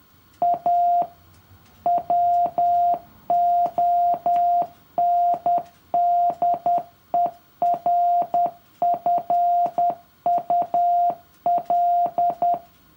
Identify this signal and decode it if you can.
Morse code, heard as a keyed tone, about 10 wpm, 700 Hz. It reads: A WONDERFUL
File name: morse-code-for-year-3